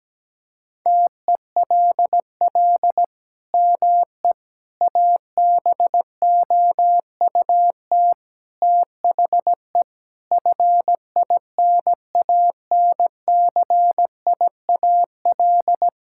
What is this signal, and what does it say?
Morse code, heard as a keyed tone, about 17 wpm, 700 Hz. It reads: TELL ME ABOUT THE FINANCIAL
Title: Sentences from Top 1000 Words - Review Part 5 - 17wpm